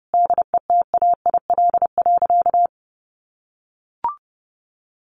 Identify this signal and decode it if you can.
Morse code, heard as a keyed tone, about 30 wpm, 700 Hz. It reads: DETAIL.
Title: Sentences from Top 1500 Words 30wpm